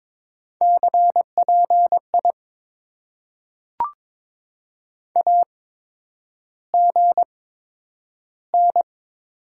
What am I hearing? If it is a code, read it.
Morse code, heard as a keyed tone, about 22 wpm, spaced as for 9 wpm, 700 Hz. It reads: CPI AGN
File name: Top 90 QSO Elements - 8x Character Spacing - 22wpm